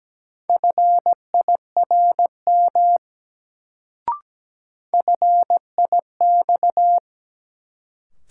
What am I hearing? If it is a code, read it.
Morse code, heard as a keyed tone, about 17 wpm, 700 Hz. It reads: FIRM FIX
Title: Sentences from Top 1400 Words - New Word Drill Alt - 17wpm